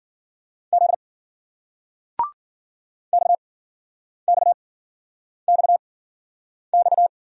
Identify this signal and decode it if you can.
Morse code, heard as a keyed tone, about 56 wpm, spaced as for 9 wpm, 700 Hz. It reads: Z XXXX